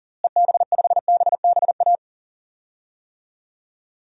Morse code, heard as a keyed tone, about 40 wpm, 700 Hz. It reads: EB5BBA
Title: DXCC-100 Call Signs - 6 Characters - Repeated 3x - 40wpm